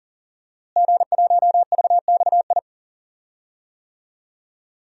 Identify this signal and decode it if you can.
Morse code, heard as a keyed tone, about 38 wpm, 700 Hz. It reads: G1VXI
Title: DXCC-100 Call Signs - 5 Characters - Repeated 3x - 40wpm